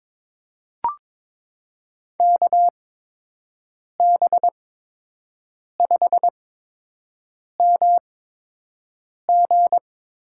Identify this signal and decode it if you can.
Morse code, heard as a keyed tone, about 22 wpm, spaced as for 6 wpm, 700 Hz. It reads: KB5MG
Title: USA Call Signs - 8x Character Spacing - 22wpm